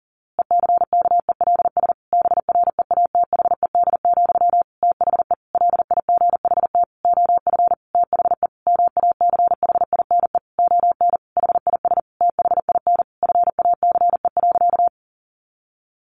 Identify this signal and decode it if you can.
Morse code, heard as a keyed tone, about 40 wpm, 700 Hz. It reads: ECKELS BREATHED, THE LIGHT OF THE MACHINE ON HIS THIN FACE.
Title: A Sound Of Thunder 40wpm